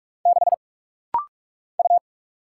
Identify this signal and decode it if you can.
Morse code, heard as a keyed tone, about 44 wpm, 690 Hz. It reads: 6 U
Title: Single Letter-Number - Rapid-Fire 45wpm